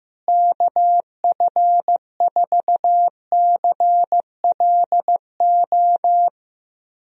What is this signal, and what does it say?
Morse code, heard as a keyed tone, about 16 wpm, 700 Hz. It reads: KF4CLO
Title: USA Call Signs - 6 Characters - Repeated 3x - 15wpm